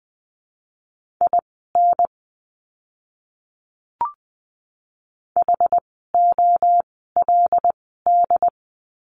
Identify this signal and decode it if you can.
Morse code, heard as a keyed tone, about 20 wpm, spaced as for 15 wpm, 700 Hz. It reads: IN HOLD
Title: 101 Lesson 17f - Letter u - Words 20wpm